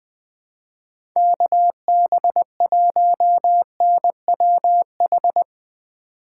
Morse code, heard as a keyed tone, about 20 wpm, 700 Hz. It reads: KB1NWH
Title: USA Call Signs - 6 Characters - 3x Character Spacing - 20wpm